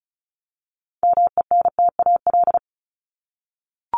Morse code, heard as a keyed tone, about 34 wpm, 700 Hz. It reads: MENTAL
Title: Top 501-750 Words 35wpm